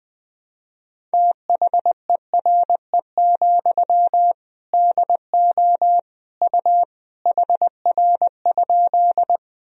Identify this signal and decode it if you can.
Morse code, heard as a keyed tone, about 20 wpm, 700 Hz. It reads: THERE, DO U HR?
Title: Sentences from Top 300 Words and Abbreviations 20wpm